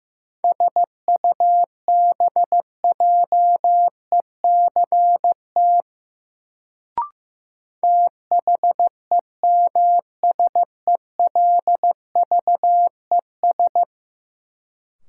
Morse code, heard as a keyed tone, about 15 wpm, 700 Hz. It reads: SUBJECT THEMSELVES
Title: Sentences from Top 800 Words - New Word Drill - 15wpm